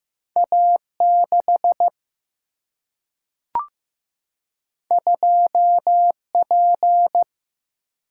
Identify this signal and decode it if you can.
Morse code, heard as a keyed tone, about 15 wpm, 700 Hz. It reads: A 6 2 P
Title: Single-Letter Single-Number 15wpm